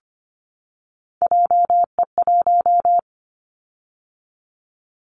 Morse code, heard as a keyed tone, about 25 wpm, 700 Hz. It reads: JE1
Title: Callsign Prefixes for Top 25 DXCC - 25wpm